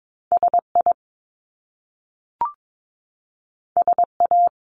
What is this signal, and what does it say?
Morse code, heard as a keyed tone, about 22 wpm, 700 Hz. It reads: SI SA
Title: Top Two Letter Combinations in Words 22wpm